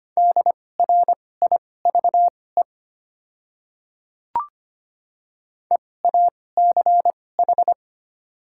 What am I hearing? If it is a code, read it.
Morse code, heard as a keyed tone, about 25 wpm, spaced as for 18 wpm, 700 Hz. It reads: DRIVE EACH
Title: 2x Character Spacing - Top 500 Words - 25wpm